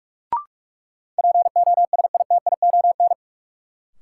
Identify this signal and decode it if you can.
Morse code, heard as a keyed tone, about 44 wpm, 700 Hz. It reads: POSITION
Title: Sentences from Top 900 Words - New Word Drill - 45wpm